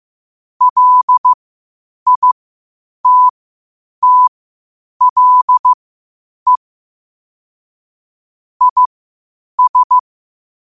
Morse code, heard as a keyed tone, about 15 wpm, spaced as for 8 wpm, 1000 Hz. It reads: LITTLE IS